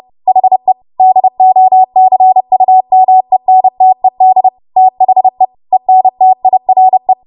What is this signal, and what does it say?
Morse code, heard as a keyed tone, about 30 wpm, 750 Hz. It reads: HE DOCUMENTED THE ENTIRE
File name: Morse / 30WPM